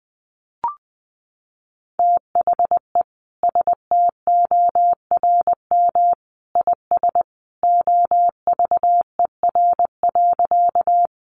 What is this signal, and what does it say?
Morse code, heard as a keyed tone, about 20 wpm, 700 Hz. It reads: THE STORM IS OVER.
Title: Sentences from Top 1500 Words 20wpm